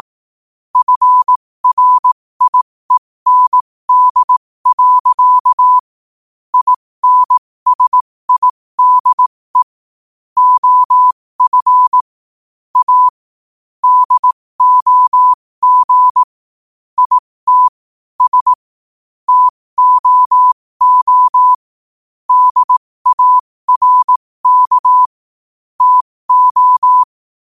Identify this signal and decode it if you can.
Morse code, heard as a keyed tone, about 18 wpm, spaced as for 15 wpm, 1000 Hz. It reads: FRIEND. INSIDE OF A DOG IT S TOO DARK TO
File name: qotdfiles / 15wpm